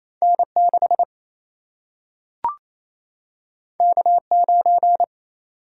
Morse code, heard as a keyed tone, about 28 wpm, 700 Hz. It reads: N6 K9
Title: Single-Letter Single-Number 28wpm